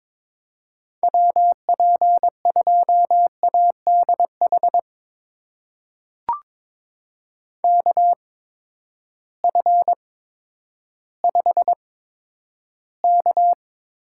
Morse code, heard as a keyed tone, about 22 wpm, spaced as for 11 wpm, 700 Hz. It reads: WP2ADH KF5K
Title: USA Call Signs - 6 Characters - 8x Character Spacing - 22wpm